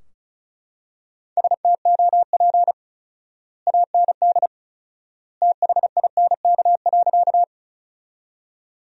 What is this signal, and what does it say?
Morse code, heard as a keyed tone, about 34 wpm, 700 Hz. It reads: STOP AND THINK.
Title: 4x Word-Spacing - Sentences from Top 300 Words - 35wpm